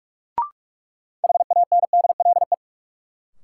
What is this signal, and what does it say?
Morse code, heard as a keyed tone, about 46 wpm, 700 Hz. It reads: HANDLE
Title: Sentences from Top 1100 Words - New Word Drill Alt - 45wpm